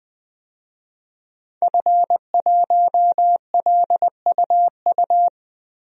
Morse code, heard as a keyed tone, about 21 wpm, 700 Hz. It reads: F1LUU
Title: DXCC-100 Call Signs - 5 Characters - 20wpm